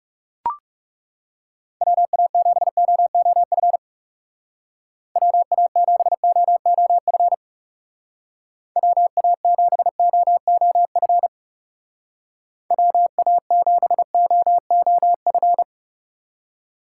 Morse code, heard as a keyed tone, about 37 wpm, 700 Hz. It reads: WA7OOF WA7OOF WA7OOF WA7OOF
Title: Speed Racing - USA Call Signs 30wpm